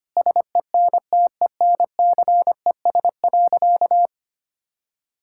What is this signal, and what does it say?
Morse code, heard as a keyed tone, about 25 wpm, 700 Hz. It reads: SENTENCES.
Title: Pangram Challenge v2 - 25wpm